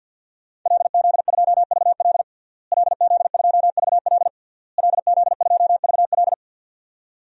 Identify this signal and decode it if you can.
Morse code, heard as a keyed tone, about 51 wpm, 700 Hz. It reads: LZ2VL LZ2VL LZ2VL
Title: DXCC-100 Call Signs - 5 Characters - Repeated 3x - 50wpm